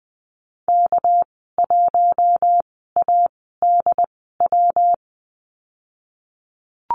Morse code, heard as a keyed tone, about 20 wpm, spaced as for 15 wpm, 700 Hz. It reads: K1ADW
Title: 219 Lesson 35g - Number 0 - Call Signs 20wpm